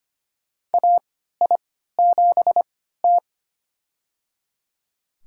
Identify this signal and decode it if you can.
Morse code, heard as a keyed tone, about 25 wpm, spaced as for 14 wpm, 700 Hz. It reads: AI7T
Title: USA Call Signs - 4 Characters - 3x Character Spacing - 25wpm